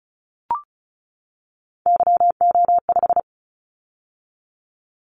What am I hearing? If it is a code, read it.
Morse code, heard as a keyed tone, about 36 wpm, 700 Hz. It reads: YO5
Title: Callsign Prefixes for Top 25 DXCC - 35wpm